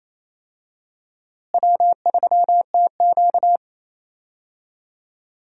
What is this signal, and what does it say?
Morse code, heard as a keyed tone, about 27 wpm, 700 Hz. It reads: W3TQ
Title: USA Call Signs - 4 Characters - Repeated 2x - 28wpm